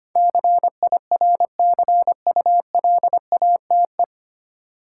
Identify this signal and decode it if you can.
Morse code, heard as a keyed tone, about 25 wpm, 700 Hz. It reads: CIRCULATE
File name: practice / 25wpm / 9 Letter Words - First 3 Letters Given - 25wpm